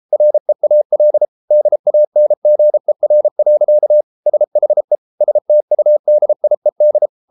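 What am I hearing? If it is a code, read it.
Morse code, heard as a keyed tone, about 33 wpm, 600 Hz. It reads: REAL DANGER. SHE STUDIED
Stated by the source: Morse code practice at 32 wpm from CW med Gnister, reading Ken Follett's novel Never in English.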